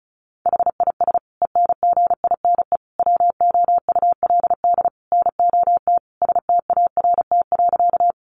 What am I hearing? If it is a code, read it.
Morse code, heard as a keyed tone, about 34 wpm, 700 Hz. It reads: HIS ENGINE WOULD NOT START.
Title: Sentences from Top 1500 Words 35wpm